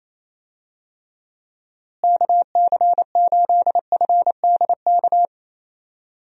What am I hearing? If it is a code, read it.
Morse code, heard as a keyed tone, about 27 wpm, 700 Hz. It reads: KC8FDK
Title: USA Call Signs - 6 Characters - Repeated 3x - 28wpm